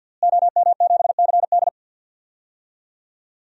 Morse code, heard as a keyed tone, about 53 wpm, 700 Hz. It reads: OM7CD
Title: DXCC-100 Call Signs - 5 Characters - Repeated 3x - 50wpm